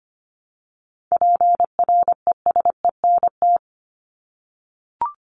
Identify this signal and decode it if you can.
Morse code, heard as a keyed tone, about 25 wpm, 700 Hz. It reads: PRESENT
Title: ED Suffix - Top 100 Words with Base Words 25wpm